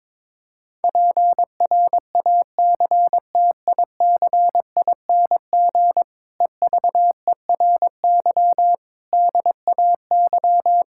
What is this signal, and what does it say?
Morse code, heard as a keyed tone, about 22 wpm, 700 Hz. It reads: PRACTICING EVERY DAY
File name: Difficult Sentences with Hints - 22wpm